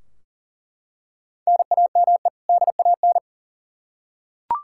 Morse code, heard as a keyed tone, about 40 wpm, 700 Hz. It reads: NAME DAN
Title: Practice QSO - Exchange 1 - Beginning - 40wpm